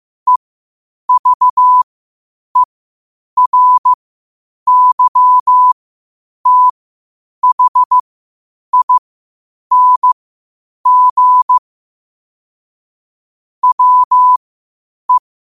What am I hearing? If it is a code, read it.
Morse code, heard as a keyed tone, about 15 wpm, spaced as for 8 wpm, 1000 Hz. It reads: EVERYTHING WE
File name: qotdfiles / 8wpm